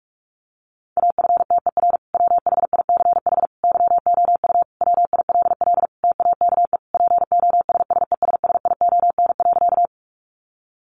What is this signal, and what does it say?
Morse code, heard as a keyed tone, about 44 wpm, 700 Hz. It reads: AFTER WHICH YOU WILL TAKE POSSESSION.
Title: Sentences from Top 1400 Words 45wpm